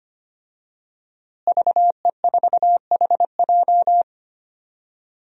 Morse code, heard as a keyed tone, about 25 wpm, 700 Hz. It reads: VE4HJ
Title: DXCC-100 Call Signs - 5 Characters - Repeated 2x - 25wpm